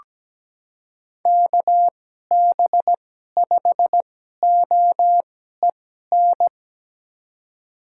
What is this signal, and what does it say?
Morse code, heard as a keyed tone, about 17 wpm, spaced as for 12 wpm, 700 Hz. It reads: KB5OEN
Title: USA Call Signs - 6 Characters - 2x Character Spacing - 17wpm